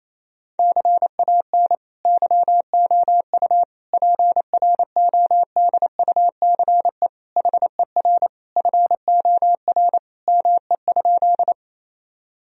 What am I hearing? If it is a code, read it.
Morse code, heard as a keyed tone, about 28 wpm, 700 Hz. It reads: CAN YOU PRODUCE HER FOR ME?
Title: Sentences from Top 1000 Words - Review Part 3 - 28wpm